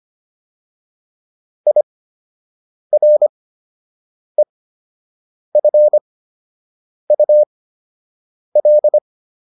Morse code, heard as a keyed tone, about 25 wpm, spaced as for 7 wpm, 600 Hz. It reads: IREFUL
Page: fundamental-practice-files